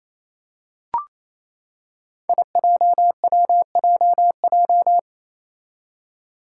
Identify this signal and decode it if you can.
Morse code, heard as a keyed tone, about 28 wpm, 700 Hz. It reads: IJWJJ
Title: Five Challenging Letter-Number Combinations 28wpm